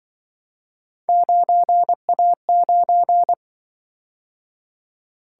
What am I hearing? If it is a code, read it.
Morse code, heard as a keyed tone, about 25 wpm, 700 Hz. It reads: 9A9